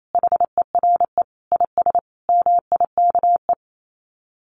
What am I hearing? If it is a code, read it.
Morse code, heard as a keyed tone, about 28 wpm, 700 Hz. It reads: HERE IS MIKE
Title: Common QSO Phrases 28wpm